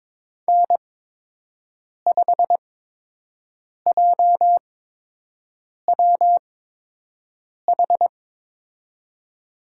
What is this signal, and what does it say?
Morse code, heard as a keyed tone, about 22 wpm, spaced as for 6 wpm, 700 Hz. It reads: N5JWH
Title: USA Call Signs - 5 Characters - 8x Character Spacing - 22wpm